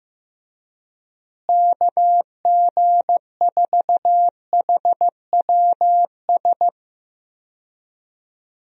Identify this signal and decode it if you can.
Morse code, heard as a keyed tone, about 15 wpm, 700 Hz. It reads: KG4HWS
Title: USA Call Signs - 6 Characters - Repeated 2x - 15wpm